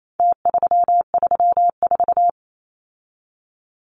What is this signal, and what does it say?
Morse code, heard as a keyed tone, about 28 wpm, 700 Hz. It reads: T334
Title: Numbers - Contest Style - 4-Digit 28wpm